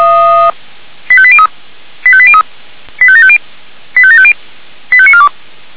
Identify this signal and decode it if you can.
Five-tone selective call (ZVEI), heard as quick runs of five tones, tones of about 70 ms. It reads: 85803 85803 85750 85750 85942